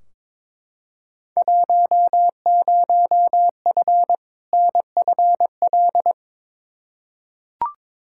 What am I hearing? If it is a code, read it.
Morse code, heard as a keyed tone, about 21 wpm, 700 Hz. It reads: 10F NFL
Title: Field Day Exchanges Encore - 22wpm